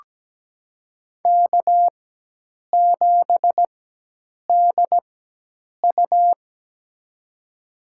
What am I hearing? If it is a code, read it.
Morse code, heard as a keyed tone, about 17 wpm, spaced as for 8 wpm, 700 Hz. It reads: K7DU
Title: USA Call Signs - 4x Character Spacing - 17wpm